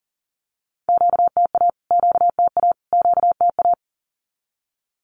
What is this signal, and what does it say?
Morse code, heard as a keyed tone, about 40 wpm, 700 Hz. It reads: QTA QTA QTA
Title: Q-Codes Repeated 3x 40wpm